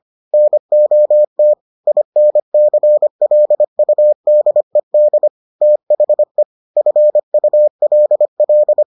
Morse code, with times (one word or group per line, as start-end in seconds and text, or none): 0.3-1.5 NOT
1.9-5.3 INCLUDED
5.6-6.4 THE
6.8-8.8 FULL